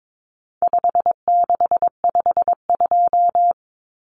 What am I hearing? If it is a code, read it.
Morse code, heard as a keyed tone, about 22 wpm, 700 Hz. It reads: 5652
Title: Common Cut Numbers - 4-Digit 22wpm